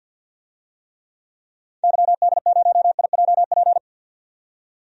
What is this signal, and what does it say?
Morse code, heard as a keyed tone, about 51 wpm, 700 Hz. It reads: YD0IJP
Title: DXCC-100 Call Signs - 6 Characters - 50wpm